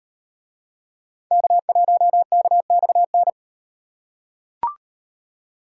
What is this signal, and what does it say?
Morse code, heard as a keyed tone, about 34 wpm, 700 Hz. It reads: K1KXN